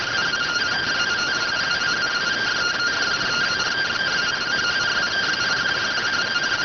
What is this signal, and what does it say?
Two-tone teleprinter signal, about 170 Hz shift.